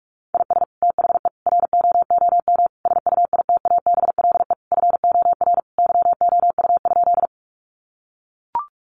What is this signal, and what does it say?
Morse code, heard as a keyed tone, about 44 wpm, 700 Hz. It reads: IS THE ROOM SUITABLE FOR YOU?
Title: Sentences from Top 1400 Words 45wpm